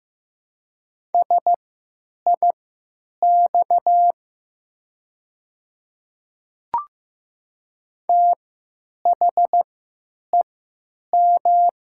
Morse code, heard as a keyed tone, about 15 wpm, spaced as for 8 wpm, 700 Hz. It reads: SIX THEM
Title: 3x Character Spacing - Top 500 Words - 15wpm